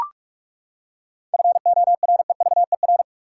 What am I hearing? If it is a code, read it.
Morse code, heard as a keyed tone, about 44 wpm, 700 Hz. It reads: FOREVER
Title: Seven Letter Words - 45wpm